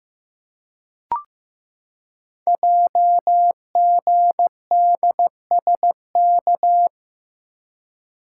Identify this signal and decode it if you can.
Morse code, heard as a keyed tone, about 15 wpm, 700 Hz. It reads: JGDSK
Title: Code Groups - 5 Letter:Numbers - 15wpm